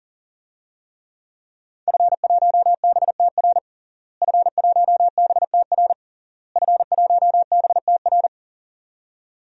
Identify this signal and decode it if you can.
Morse code, heard as a keyed tone, about 40 wpm, 700 Hz. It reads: F1BTR F1BTR F1BTR
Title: DXCC-100 Call Signs - 5 Characters - Repeated 3x - 40wpm